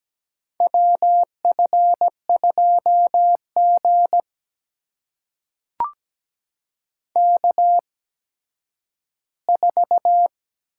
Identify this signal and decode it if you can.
Morse code, heard as a keyed tone, about 17 wpm, spaced as for 10 wpm, 700 Hz. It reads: WF2G K4
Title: USA Call Signs - 8x Character Spacing - 17wpm